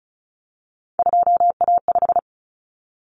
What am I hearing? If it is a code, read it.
Morse code, heard as a keyed tone, about 34 wpm, 700 Hz. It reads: 2A5
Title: Common Cut Numbers - 3-Digit 35wpm